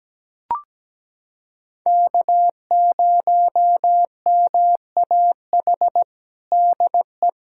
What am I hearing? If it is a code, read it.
Morse code, heard as a keyed tone, about 17 wpm, 700 Hz. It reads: K0MAH DE
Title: USA Callsign DE Callsign 17wpm